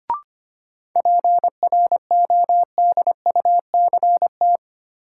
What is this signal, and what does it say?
Morse code, heard as a keyed tone, about 25 wpm, 700 Hz. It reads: PRODUCT
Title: Sentences from Top 1000 Words - Review Part 2 - New Word Drill - 25wpm